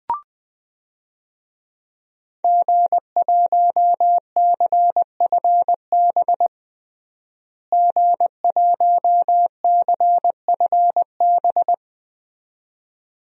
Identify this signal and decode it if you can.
Morse code, heard as a keyed tone, about 21 wpm, 700 Hz. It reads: G1CFB G1CFB
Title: DXCC-100 Call Signs - 5 Characters - Repeated 2x - 20wpm